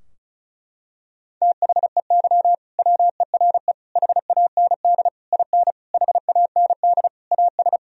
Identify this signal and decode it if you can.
Morse code, heard as a keyed tone, about 36 wpm, 700 Hz. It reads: THEY WERE HAND IN HAND AS
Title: Sentences from Top 1000 Words - Review Part 2 - 35wpm